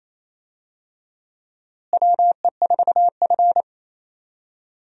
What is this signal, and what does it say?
Morse code, heard as a keyed tone, about 28 wpm, 700 Hz. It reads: WE4F
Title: USA Call Signs - 4 Characters - Repeated 3x - 28wpm